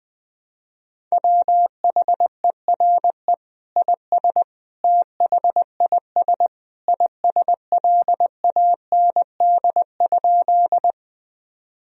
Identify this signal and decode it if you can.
Morse code, heard as a keyed tone, about 20 wpm, 700 Hz. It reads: WHERE IS THIS ISLAND?
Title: Sentences from Top 1000 Words - Review Part 3 - 20wpm